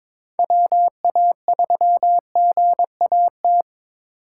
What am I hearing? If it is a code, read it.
Morse code, heard as a keyed tone, about 26 wpm, 700 Hz. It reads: WA3GAT
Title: Speed Racing - USA 2x3 Call Signs 22wpm